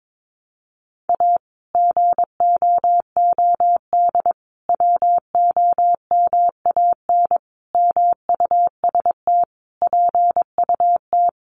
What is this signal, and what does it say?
Morse code, heard as a keyed tone, about 22 wpm, 700 Hz. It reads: A GOOD WOMAN MUST PUT
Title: Sentences from Top 1100 Words 22wpm